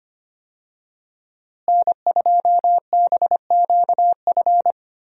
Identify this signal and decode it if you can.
Morse code, heard as a keyed tone, about 25 wpm, 700 Hz. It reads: N2BQF
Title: USA Call Signs - 5 Characters - Repeated 3x - 25wpm